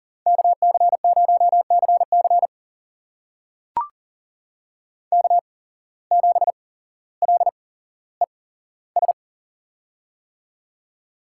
Morse code, heard as a keyed tone, about 38 wpm, spaced as for 17 wpm, 700 Hz. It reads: KC0CC K7LES
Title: USA Call Signs - 5 Characters - 8x Character Spacing - 40wpm